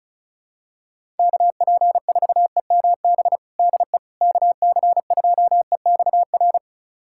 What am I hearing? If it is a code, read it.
Morse code, heard as a keyed tone, about 36 wpm, 700 Hz. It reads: KP4EMB DE KC2EXR
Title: USA Callsign DE Callsign 35wpm